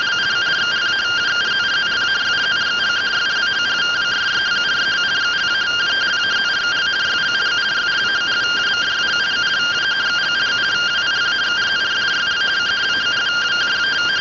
Two-tone teleprinter signal, about 170 Hz shift.